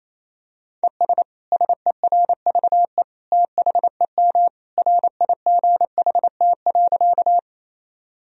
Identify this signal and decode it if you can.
Morse code, heard as a keyed tone, about 27 wpm, 700 Hz. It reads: ES SERVE THEM RIGHT.
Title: Sentences from Top 400 Words and Abbreviations 28wpm